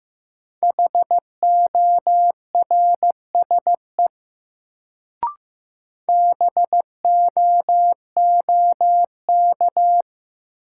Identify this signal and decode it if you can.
Morse code, heard as a keyed tone, about 15 wpm, 700 Hz. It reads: HORSE BOOK
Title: Sentences from Top 300 Words - New Word Drill - 15wpm